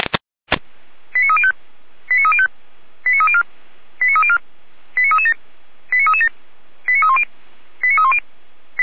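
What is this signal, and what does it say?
Five-tone selective call (ZVEI), heard as quick runs of five tones, tones of about 80 ms. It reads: 89305 89305 89304 89304 89337 89337 89310 89310
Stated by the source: ZVEI five-tone selective call 89229, received on an SDR.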